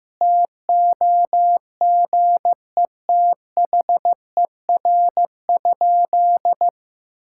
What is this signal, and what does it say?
Morse code, heard as a keyed tone, about 16 wpm, 700 Hz. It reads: TOGETHER?
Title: Sentences from Top 1600 Words 15wpm